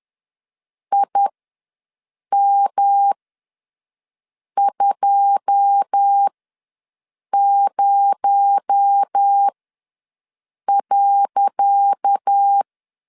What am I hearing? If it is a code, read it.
Morse code, heard as a keyed tone, about 10 wpm, spaced as for 5 wpm, 800 Hz. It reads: IM20.
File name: LEZIONI / 10 SIGNS / 10C DOT ws 10wpm 4213